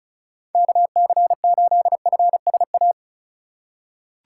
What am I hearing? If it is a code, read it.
Morse code, heard as a keyed tone, about 34 wpm, 700 Hz. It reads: KC8FSA
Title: USA Call Signs - 6 Characters - 2x Character Spacing - 35wpm